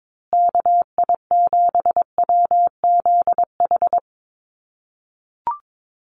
Morse code, heard as a keyed tone, about 22 wpm, 700 Hz. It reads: KI7WZH
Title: USA Call Signs Repeated 2x 22wpm